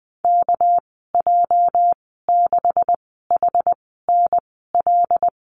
Morse code, heard as a keyed tone, about 20 wpm, spaced as for 15 wpm, 700 Hz. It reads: KJ6HNL
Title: 231 Lesson 37g - Letter x - Call Signs 20wpm